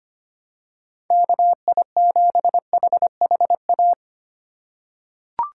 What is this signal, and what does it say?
Morse code, heard as a keyed tone, about 25 wpm, 700 Hz. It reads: KI7HHA
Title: USA Call Signs - 6 Characters - 3x Character Spacing - 25wpm